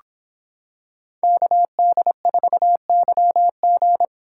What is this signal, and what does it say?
Morse code, heard as a keyed tone, about 26 wpm, 700 Hz. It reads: KD4YG